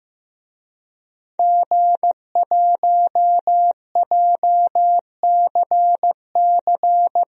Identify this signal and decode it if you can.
Morse code, heard as a keyed tone, about 15 wpm, 700 Hz. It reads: G1JCC
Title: DXCC-100 Call Signs - 5 Characters - 15wpm